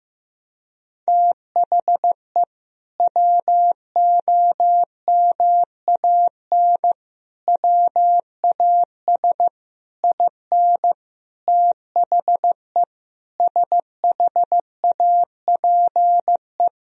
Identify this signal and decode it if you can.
Morse code, heard as a keyed tone, about 15 wpm, 700 Hz. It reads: THE WOMAN WAS IN THE SHAPE
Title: Sentences from Top 1000 Words - Review Part 3 - 15wpm